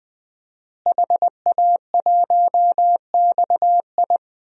Morse code, heard as a keyed tone, about 21 wpm, 700 Hz. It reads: HA1XI
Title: DXCC-100 Call Signs - 5 Characters - Repeated 3x - 20wpm